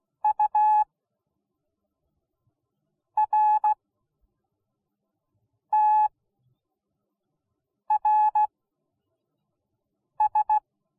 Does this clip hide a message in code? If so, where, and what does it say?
Yes: Morse code from 0.2 to 10.6 s, heard as a keyed tone, about 16 wpm, spaced as for 4 wpm, 850 Hz. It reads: URTRS